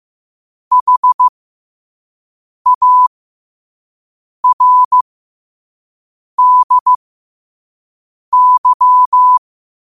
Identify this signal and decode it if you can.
Morse code, heard as a keyed tone, about 15 wpm, spaced as for 5 wpm, 1000 Hz. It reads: HARDY
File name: qotdfiles / 5wpm